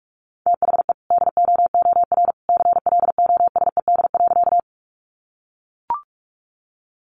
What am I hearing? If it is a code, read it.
Morse code, heard as a keyed tone, about 44 wpm, 700 Hz. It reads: THE DOOR CLOSED.
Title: Sentences from Top 1400 Words 45wpm